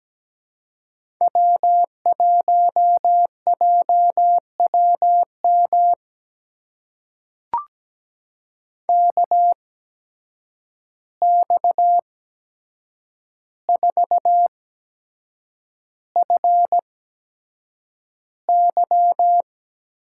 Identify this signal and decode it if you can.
Morse code, heard as a keyed tone, about 17 wpm, spaced as for 7 wpm, 700 Hz. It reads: W1JWM KX4FY
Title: USA Call Signs - 5 Characters - 8x Character Spacing - 17wpm